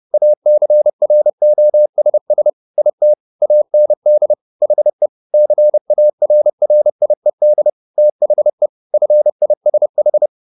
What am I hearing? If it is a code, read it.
Morse code, heard as a keyed tone, about 30 wpm, 600 Hz. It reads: ACROSS IT AND HE CARRIED THE FISH